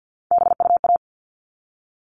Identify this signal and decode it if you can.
Morse code, heard as a keyed tone, about 50 wpm, 700 Hz. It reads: BUA
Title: All Cut Numbers - 3-Digit 50wpm